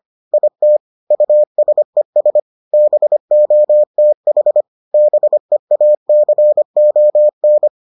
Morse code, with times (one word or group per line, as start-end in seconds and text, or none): 0.3-0.8 IT
1.1-2.4 USES
2.7-4.6 BOTH
4.9-7.7 BEACON